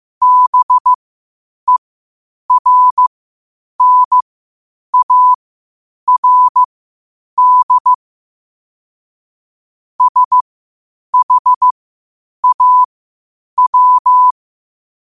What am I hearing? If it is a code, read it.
Morse code, heard as a keyed tone, about 15 wpm, spaced as for 8 wpm, 1000 Hz. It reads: BERNARD SHAW
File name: qotdfiles / 8wpm